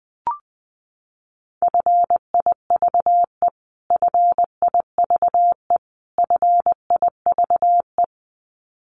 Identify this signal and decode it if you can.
Morse code, heard as a keyed tone, about 20 wpm, 700 Hz. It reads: FIVE FIVE FIVE
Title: Top 500 Words Repeated 20wpm